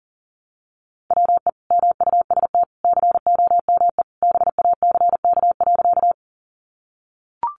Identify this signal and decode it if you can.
Morse code, heard as a keyed tone, about 40 wpm, 700 Hz. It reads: WE MUST COME BACK.
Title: Sentences from Top 200 Words 40wpm